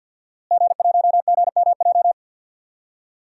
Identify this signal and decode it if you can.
Morse code, heard as a keyed tone, about 46 wpm, 700 Hz. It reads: G1GMJ
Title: DXCC-100 Call Signs - 5 Characters - Repeated 2x - 50wpm